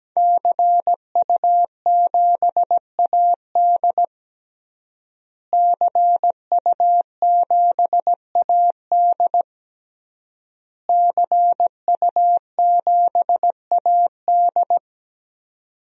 Morse code, heard as a keyed tone, about 17 wpm, 700 Hz. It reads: CU7AD CU7AD CU7AD
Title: DXCC-100 Call Signs - 5 Characters - Repeated 3x - 17wpm